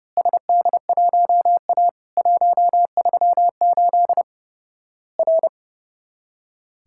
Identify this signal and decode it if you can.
Morse code, heard as a keyed tone, about 30 wpm, 700 Hz. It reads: SD1A 138 R
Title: Worked All Europe DX Contest - QTC Exchange - 30wpm